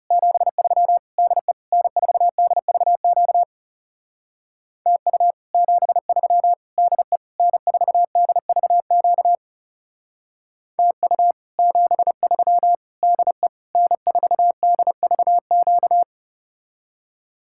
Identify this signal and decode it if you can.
Morse code, heard as a keyed tone, about 39 wpm, 700 Hz. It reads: 73 DE N4DVQ TU 73 DE N4DVQ TU 73 DE N4DVQ